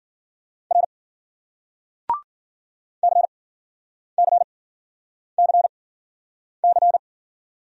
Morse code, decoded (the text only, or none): R CCCC